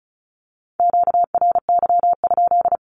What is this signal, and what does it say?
Morse code, heard as a keyed tone, about 36 wpm, 700 Hz. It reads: QRY?
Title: Q-Codes with Spoken Meaning - 35wpm